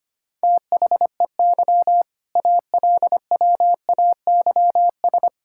Morse code, heard as a keyed tone, about 25 wpm, 700 Hz. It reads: THEY ALWAYS
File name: practice / 25wpm / All-Age Appropriate Jokes - 25wpm